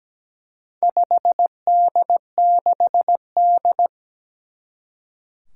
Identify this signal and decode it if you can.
Morse code, heard as a keyed tone, about 17 wpm, 700 Hz. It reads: 5D6D
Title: Four Challenging Letter-Number Combinations 17wpm